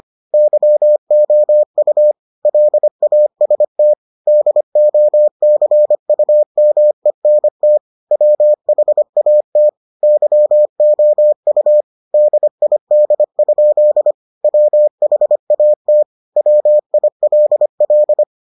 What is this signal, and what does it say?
Morse code, heard as a keyed tone, about 25 wpm, 600 Hz. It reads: YOU LAST DOCUMENT WHAT YOU DID? WHAT WILL